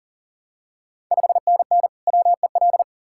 Morse code, heard as a keyed tone, about 40 wpm, 700 Hz. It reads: 5NN WEL
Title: Ontario QSO Party Exchange 40wpm